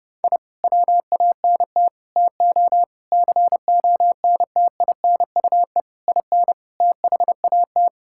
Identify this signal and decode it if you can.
Morse code, heard as a keyed tone, about 30 wpm, 700 Hz. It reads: I WANT TO CONTINUE IN THAT
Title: Sentences from Top 1000 Words - Review Part 6 - 30wpm